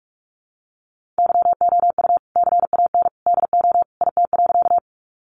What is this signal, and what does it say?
Morse code, heard as a keyed tone, about 44 wpm, 700 Hz. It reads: YOU CAN DO IT.
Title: Sentences from Top 100 Words 45wpm